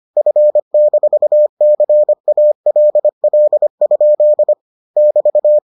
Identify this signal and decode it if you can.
Morse code, heard as a keyed tone, about 25 wpm, 600 Hz. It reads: F-CALL? <BT>